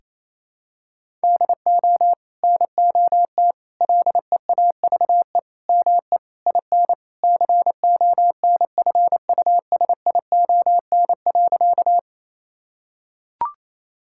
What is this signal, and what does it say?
Morse code, heard as a keyed tone, about 28 wpm, 700 Hz. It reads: DO NOT LEAVE ME IN CONFUSION.
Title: Sentences from Top 1600 Words 28wpm